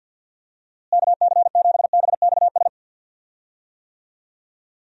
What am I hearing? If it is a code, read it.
Morse code, heard as a keyed tone, about 47 wpm, 700 Hz. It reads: KC7BXS
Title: USA Call Signs - 6 Characters - Repeated 3x - 50wpm